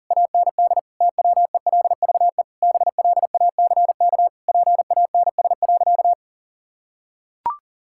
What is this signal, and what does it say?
Morse code, heard as a keyed tone, about 40 wpm, 700 Hz. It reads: AND TWELVE BLACK PANS.
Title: Pangram Challenge v2 - 40wpm